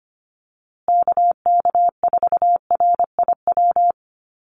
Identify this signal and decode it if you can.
Morse code, heard as a keyed tone, about 25 wpm, 700 Hz. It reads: KK4RIW
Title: USA Call Signs Repeated 3x 25wpm